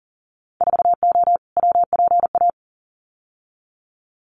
Morse code, heard as a keyed tone, about 40 wpm, 700 Hz. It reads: 4O WPA
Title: Winter Field Day Exchanges 40wpm